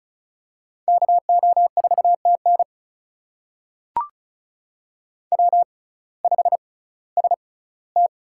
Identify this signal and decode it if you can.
Morse code, heard as a keyed tone, about 36 wpm, spaced as for 19 wpm, 700 Hz. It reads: KO4TN W5ST
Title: USA Call Signs - 5 Characters - 6x Character Spacing - 35wpm